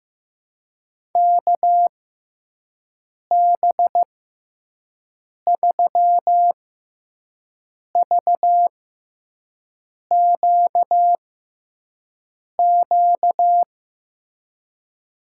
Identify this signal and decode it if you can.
Morse code, heard as a keyed tone, about 16 wpm, spaced as for 5 wpm, 700 Hz. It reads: KB3VQQ
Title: USA Call Signs - 6x Character Spacing - 15wpm